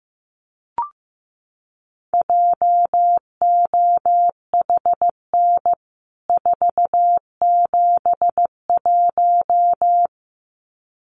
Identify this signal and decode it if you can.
Morse code, heard as a keyed tone, about 15 wpm, 700 Hz. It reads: JOHN 471
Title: CWT Exchange 15wpm